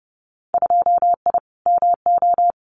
Morse code, heard as a keyed tone, about 30 wpm, 700 Hz. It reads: 2I MO
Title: Winter Field Day Exchanges 30wpm